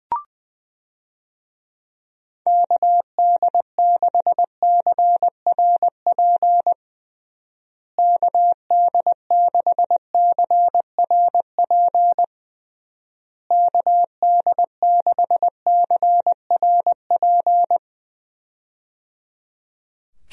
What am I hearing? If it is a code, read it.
Morse code, heard as a keyed tone, about 20 wpm, 700 Hz. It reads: KD6CRP KD6CRP KD6CRP
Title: USA Call Signs - 6 Characters - Repeated 3x - 20wpm